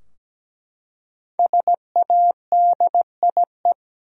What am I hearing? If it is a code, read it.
Morse code, heard as a keyed tone, about 17 wpm, 700 Hz. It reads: SADIE
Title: USA Common Names - Max 5 Letters - 17wpm